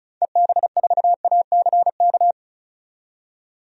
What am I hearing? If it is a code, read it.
Morse code, heard as a keyed tone, about 36 wpm, 700 Hz. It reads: EB4ACK
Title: DXCC-100 Call Signs - 6 Characters - 35wpm